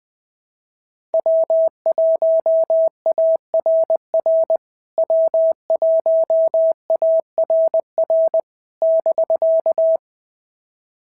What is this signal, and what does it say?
Morse code, heard as a keyed tone, about 20 wpm, 650 Hz. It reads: W1ARR W1ARR <BK>
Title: POTA QSO Practice - 20wpm